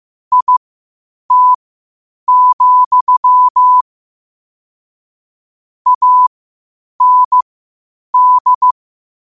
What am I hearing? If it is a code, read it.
Morse code, heard as a keyed tone, about 15 wpm, spaced as for 8 wpm, 1000 Hz. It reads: IT, AND
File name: qotdfiles / 8wpm